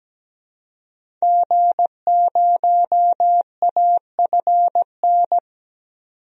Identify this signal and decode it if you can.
Morse code, heard as a keyed tone, about 17 wpm, 700 Hz. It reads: G0AFN
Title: DXCC-100 Call Signs - 5 Characters - 17wpm